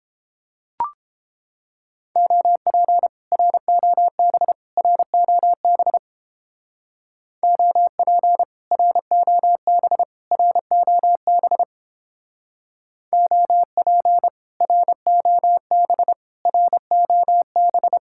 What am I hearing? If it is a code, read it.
Morse code, heard as a keyed tone, about 29 wpm, 700 Hz. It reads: OP ROB ROB OP ROB ROB OP ROB ROB